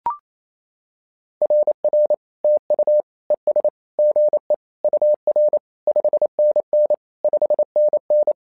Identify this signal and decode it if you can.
Morse code, heard as a keyed tone, about 27 wpm, 600 Hz. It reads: RR TU ES GE UR 5NN 5NN
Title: SOTA QSO Practice 28wpm